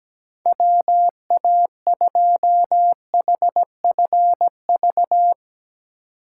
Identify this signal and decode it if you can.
Morse code, heard as a keyed tone, about 17 wpm, 700 Hz. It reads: WA2HFV
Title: USA Call Signs - 6 Characters - 2x Character Spacing - 17wpm